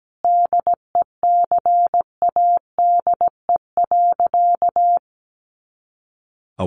Morse code, heard as a keyed tone, about 17 wpm, 700 Hz. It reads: DECADE.
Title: A Sound Of Thunder 17wpm